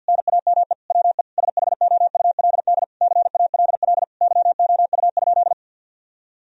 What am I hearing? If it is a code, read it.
Morse code, heard as a keyed tone, about 50 wpm, 700 Hz. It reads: NAME WE SHOULD CALL YOU?
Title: Sentences from Top 1000 Words - Review Part 3 - 50wpm